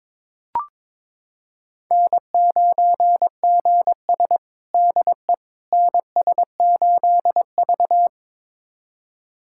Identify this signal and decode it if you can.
Morse code, heard as a keyed tone, about 22 wpm, 700 Hz. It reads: N9GS DE NS8V
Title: USA Callsign DE Callsign - 4 Characters - 22wpm